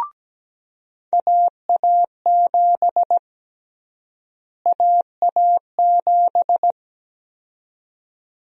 Morse code, heard as a keyed tone, about 17 wpm, 700 Hz. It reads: AA7 AA7
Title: USA Call Sign Prefixes - Repeated 2x - 17wpm